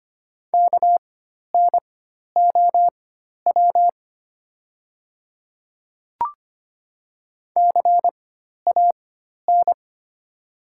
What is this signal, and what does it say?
Morse code, heard as a keyed tone, about 25 wpm, spaced as for 12 wpm, 690 Hz. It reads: KNOW CAN
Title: 4x Character Spacing - Top 100 Words - 25wpm